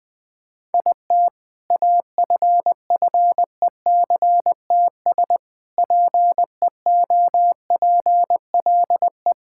Morse code, heard as a keyed tone, about 20 wpm, 700 Hz. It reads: IT AFFECTS PEOPLE
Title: Sentences with 7-Word Phrases - 20wpm